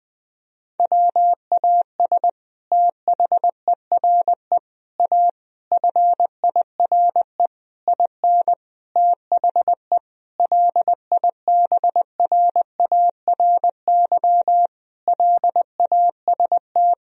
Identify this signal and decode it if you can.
Morse code, heard as a keyed tone, about 20 wpm, 700 Hz. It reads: WAS THERE A FIRE IN THE LIBRARY LAST
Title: Sentences from Top 1000 Words - Review Part 4 - 20wpm